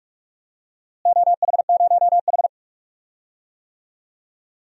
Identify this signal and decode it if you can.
Morse code, heard as a keyed tone, about 43 wpm, 700 Hz. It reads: OH0H
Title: DXCC-100 Call Signs - 4 Characters - Repeated 2x - 45wpm